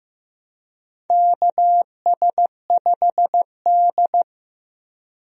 Morse code, heard as a keyed tone, about 15 wpm, 700 Hz. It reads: KS5D
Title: USA Call Signs - 4 Characters - Repeated 3x - 15wpm